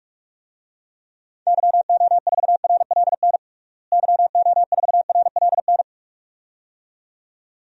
Morse code, heard as a keyed tone, about 41 wpm, 700 Hz. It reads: YO4RLN YO4RLN
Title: DXCC-100 Call Signs - 6 Characters - Repeated 2x - 45wpm